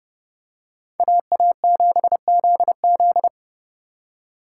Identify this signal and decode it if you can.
Morse code, heard as a keyed tone, about 30 wpm, 700 Hz. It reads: AA7ZZ